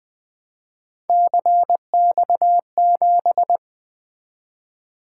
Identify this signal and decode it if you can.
Morse code, heard as a keyed tone, about 20 wpm, 700 Hz. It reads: CX7
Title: Three Letter-Number Combinations - 20wpm